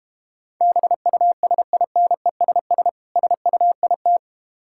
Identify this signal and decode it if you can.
Morse code, heard as a keyed tone, about 33 wpm, 700 Hz. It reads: BUSINESS SUIT